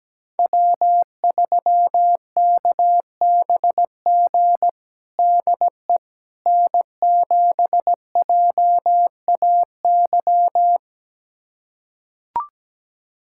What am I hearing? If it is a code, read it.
Morse code, heard as a keyed tone, about 17 wpm, 700 Hz. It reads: W3KBG DE N7JAY
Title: USA Callsign DE Callsign - 5 Characters - 17wpm